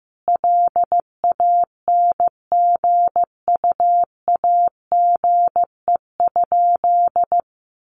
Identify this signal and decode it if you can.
Morse code, heard as a keyed tone, about 15 wpm, 700 Hz. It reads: LANGUAGE?
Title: Sentences from Top 900 Words 15wpm